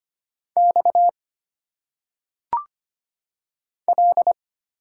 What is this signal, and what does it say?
Morse code, heard as a keyed tone, about 25 wpm, 700 Hz. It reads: X L
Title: Single Letters - Phonetic Pronunciation 25wpm